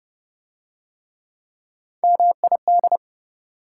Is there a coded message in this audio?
Yes: Morse code from 2.0 to 3.0 s, heard as a keyed tone, about 28 wpm, 700 Hz. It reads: MID